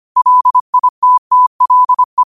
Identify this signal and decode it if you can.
Morse code, heard as a keyed tone, about 25 wpm, 1000 Hz. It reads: LITTLE
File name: qotdfiles / 25wpm